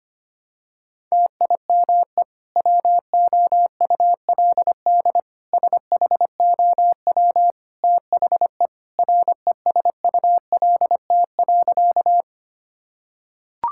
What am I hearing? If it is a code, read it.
Morse code, heard as a keyed tone, about 25 wpm, 700 Hz. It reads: TIME WOULD SHOW THE RESULT.
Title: Sentences from Top 1000 Words - Review Part 5 - 25wpm